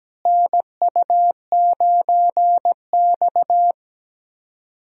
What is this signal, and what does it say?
Morse code, heard as a keyed tone, about 17 wpm, 700 Hz. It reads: NU9X
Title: USA Call Signs - 4 Characters - Repeated 3x - 17wpm